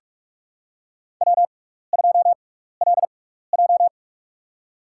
Morse code, heard as a keyed tone, about 46 wpm, spaced as for 16 wpm, 700 Hz. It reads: W2LJ
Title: USA Call Signs - 4 Characters - 6x Character Spacing - 45wpm